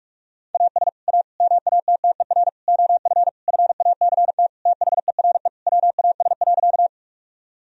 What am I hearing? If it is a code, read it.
Morse code, heard as a keyed tone, about 41 wpm, 700 Hz. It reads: AS A MATTER OF FACT THERE WAS.
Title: Sentences from Top 1000 Words - Review Part 3 - 45wpm